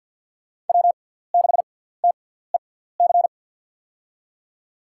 Morse code, heard as a keyed tone, about 49 wpm, spaced as for 17 wpm, 700 Hz. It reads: W6TEC
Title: USA Call Signs - 5 Characters - 6x Character Spacing - 50wpm